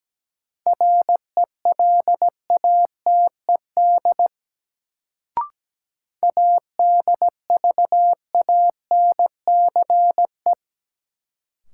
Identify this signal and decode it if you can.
Morse code, heard as a keyed tone, about 17 wpm, 700 Hz. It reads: RELATED ADVANCE
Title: Sentences from Top 1400 Words - New Word Drill Alt - 17wpm